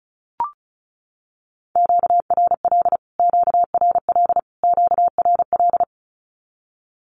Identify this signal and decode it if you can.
Morse code, heard as a keyed tone, about 34 wpm, 700 Hz. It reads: QRL QRL QRL
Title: Q-Codes Repeated 3x 35wpm